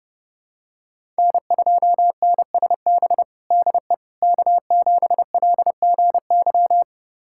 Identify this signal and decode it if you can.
Morse code, heard as a keyed tone, about 30 wpm, 700 Hz. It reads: N2NSB DE K7LGY
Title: USA Callsign DE Callsign - 5 Characters - 30wpm